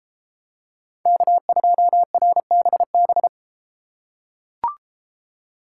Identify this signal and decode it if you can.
Morse code, heard as a keyed tone, about 32 wpm, 700 Hz. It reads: K2RBB